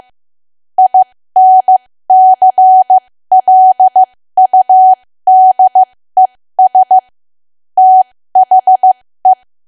Morse code, heard as a keyed tone, about 15 wpm, spaced as for 13 wpm, 750 Hz. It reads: INCLUDES THE
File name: cwa / W1AW-Level2 / 13WPM